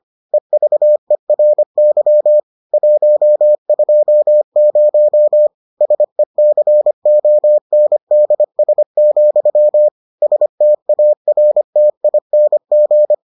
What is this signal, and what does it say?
Morse code, heard as a keyed tone, about 25 wpm, 600 Hz. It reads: EVERY 120 SECONDS, STARTING